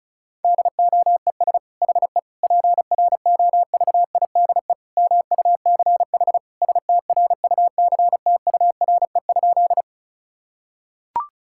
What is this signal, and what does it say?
Morse code, heard as a keyed tone, about 34 wpm, 690 Hz. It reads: DOES HE PROVIDE MUCH STRUCTURE?
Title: Sentences from Top 1000 Words - Review Part 6 - 35wpm